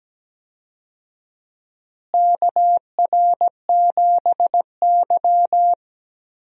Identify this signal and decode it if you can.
Morse code, heard as a keyed tone, about 17 wpm, 700 Hz. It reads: KR7Y
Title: USA Call Signs - 4 Characters - Repeated 3x - 17wpm